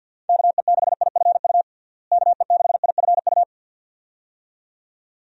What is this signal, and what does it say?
Morse code, heard as a keyed tone, about 46 wpm, 700 Hz. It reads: KE6IFU KE6IFU
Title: USA Call Signs - 6 Characters - Repeated 2x - 50wpm